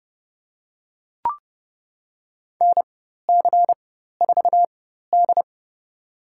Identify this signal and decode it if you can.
Morse code, heard as a keyed tone, about 31 wpm, spaced as for 14 wpm, 700 Hz. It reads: NC4D
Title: USA Call Signs - 4 Characters - 4x Character Spacing - 30wpm